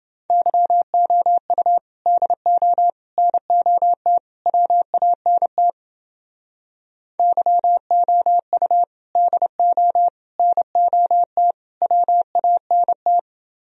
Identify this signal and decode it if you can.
Morse code, heard as a keyed tone, about 29 wpm, 700 Hz. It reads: YOU DO NOT WANT YOU DO NOT WANT